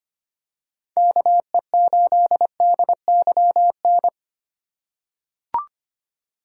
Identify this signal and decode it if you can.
Morse code, heard as a keyed tone, about 25 wpm, 700 Hz. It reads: KE8DYN
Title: USA Call Signs - 3x Character Spacing - 25wpm